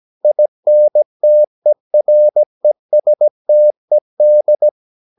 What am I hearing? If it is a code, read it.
Morse code, heard as a keyed tone, about 17 wpm, 600 Hz. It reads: INTERESTED
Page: Romanen Never af Ken Follet på engelsk.